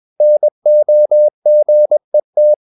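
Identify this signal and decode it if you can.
Morse code, heard as a keyed tone, about 21 wpm, 600 Hz. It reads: NOGET
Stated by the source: Morse practice at 21 wpm, slow-speed holiday exercise.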